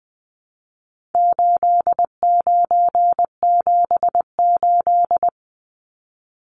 Morse code, heard as a keyed tone, about 20 wpm, 700 Hz. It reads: 8978
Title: Numbers - 4-Digit 20wpm